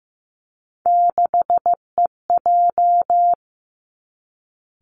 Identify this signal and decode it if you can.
Morse code, heard as a keyed tone, about 16 wpm, 700 Hz. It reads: 6EJ
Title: Three Challenging Letter-Number Combinations 15wpm